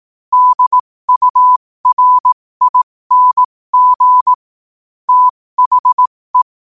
Morse code, heard as a keyed tone, about 18 wpm, spaced as for 15 wpm, 1000 Hz. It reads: DURING THE
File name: qotdfiles / 15wpm